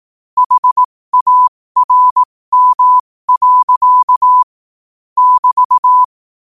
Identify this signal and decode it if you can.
Morse code, heard as a keyed tone, about 18 wpm, spaced as for 15 wpm, 1000 Hz. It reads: HARM. <BT>
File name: qotdfiles / 15wpm